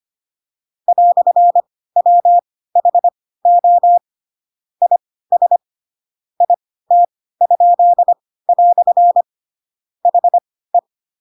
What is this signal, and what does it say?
Morse code, heard as a keyed tone, about 25 wpm, spaced as for 16 wpm, 700 Hz. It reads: "WHO IS IT?" HE
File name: Pinocchio / Pinocchio ch6 10 25 ik1hge